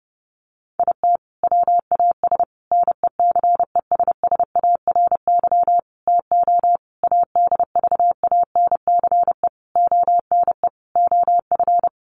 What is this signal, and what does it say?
Morse code, heard as a keyed tone, about 30 wpm, 700 Hz. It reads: IT WAS NECESSARY TO ADVANCE ONE OF
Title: Sentences from Top 1400 Words 30wpm